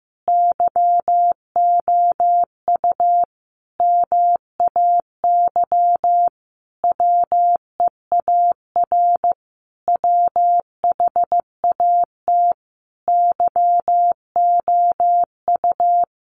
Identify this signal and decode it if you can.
Morse code, heard as a keyed tone, about 15 wpm, 700 Hz. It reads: YOU MAY WEAR WHAT YOU
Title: Sentences from Top 1500 Words 15wpm